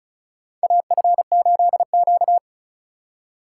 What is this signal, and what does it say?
Morse code, heard as a keyed tone, about 34 wpm, 700 Hz. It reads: AF8Q
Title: USA Call Signs - 2x Character Spacing - 35wpm